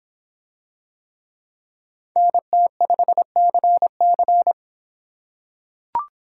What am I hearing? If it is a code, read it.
Morse code, heard as a keyed tone, about 25 wpm, 700 Hz. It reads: NT5CC